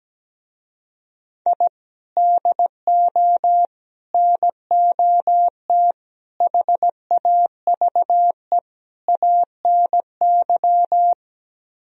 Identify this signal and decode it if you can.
Morse code, heard as a keyed tone, about 17 wpm, 700 Hz. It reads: I DO NOT HAVE ANY
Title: Speed-Racing - Sets of 5 Words 17wpm